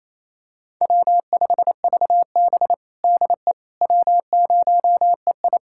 Morse code, heard as a keyed tone, about 27 wpm, 700 Hz. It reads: W5VB DE W0EI
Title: USA Callsign DE Callsign - 4 Characters - 28wpm